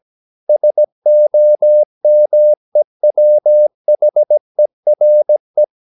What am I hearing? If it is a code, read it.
Morse code, heard as a keyed tone, about 17 wpm, 600 Hz. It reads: SOMEWHERE
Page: Romanen Never af Ken Follet på engelsk.